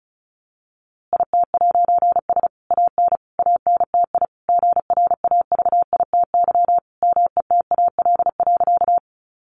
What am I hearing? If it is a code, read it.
Morse code, heard as a keyed tone, about 34 wpm, 700 Hz. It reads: IT'S AN ANTI GRAVITY METAL.
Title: A Sound Of Thunder 35wpm